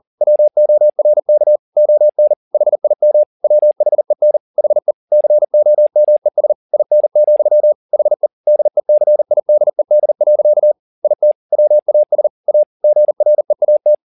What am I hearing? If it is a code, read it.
Morse code, heard as a keyed tone, about 40 wpm, 600 Hz. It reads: WORK ON HIM WHEN HE COMES IN, HE DECIDED. IT WAS A GREAT
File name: the old man and the sea 2 40wpm